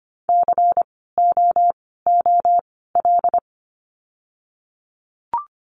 Morse code, heard as a keyed tone, about 25 wpm, 700 Hz. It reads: C O O L
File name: course / 25:10wpm / 094 Lesson 16f - Letter c - Words 25wpm